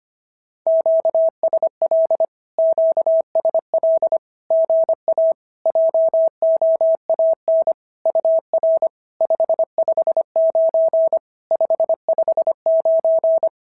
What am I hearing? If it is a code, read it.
Morse code, heard as a keyed tone, about 24 wpm, 650 Hz. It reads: QSL QSL GA JOAN UR 559 559
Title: SOTA QSO Practice 25wpm